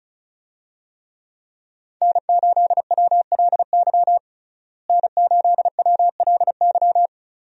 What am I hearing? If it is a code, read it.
Morse code, heard as a keyed tone, about 36 wpm, 700 Hz. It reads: N8WLY N8WLY
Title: USA Call Signs - 5 Characters - Repeated 3x - 35wpm